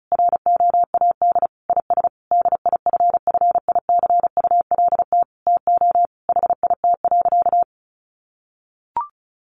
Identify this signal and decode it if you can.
Morse code, heard as a keyed tone, about 34 wpm, 700 Hz. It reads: ROAD IS DIFFICULT TO HIT.
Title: Sentences from Top 1200 Words 35wpm